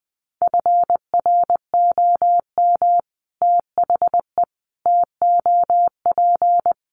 Morse code, heard as a keyed tone, about 20 wpm, 700 Hz. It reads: FROM THE TOP
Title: Sets of 5 Words Encore 20wpm